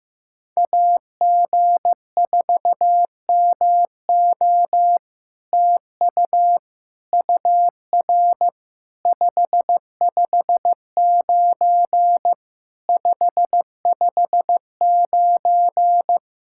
Morse code, heard as a keyed tone, about 15 wpm, 700 Hz. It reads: AG4MO TU UR 559 559
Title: SOTA QSO Practice 15wpm